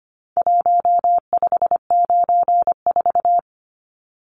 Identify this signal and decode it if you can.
Morse code, heard as a keyed tone, about 25 wpm, 700 Hz. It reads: 1594
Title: Numbers - Contest Style - 4-Digit 25wpm